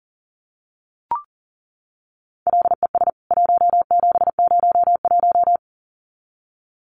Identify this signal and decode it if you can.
Morse code, heard as a keyed tone, about 40 wpm, 700 Hz. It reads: LES 1701
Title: CWT Exchange 40wpm